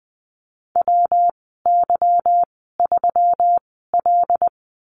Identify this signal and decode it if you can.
Morse code, heard as a keyed tone, about 20 wpm, 700 Hz. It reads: W Y 3 L
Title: 154 Lesson 25g - Letter p - Call Signs 20wpm